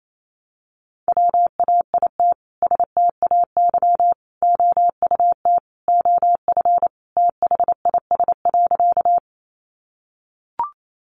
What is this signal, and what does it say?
Morse code, heard as a keyed tone, about 28 wpm, 700 Hz. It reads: WAIT STAY OUT OF THIS.
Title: A Sound Of Thunder 28wpm